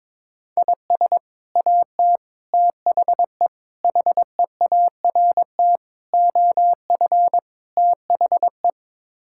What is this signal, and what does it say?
Morse code, heard as a keyed tone, about 22 wpm, 700 Hz. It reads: IS AT THE HEART OF THE
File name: practice / 22wpm / Sets of 6 Words - 22wpm